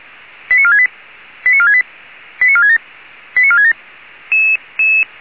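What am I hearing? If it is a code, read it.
Five-tone selective call (ZVEI), heard as quick runs of five tones, tones of about 70 ms. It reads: 89468 89468 89467 89467
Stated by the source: ZVEI five-tone selective call received by SDR.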